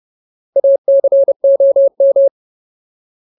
Morse code, heard as a keyed tone, about 30 wpm, 550 Hz. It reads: ACOM
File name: Ultimate Word List / 30WPM Ultimate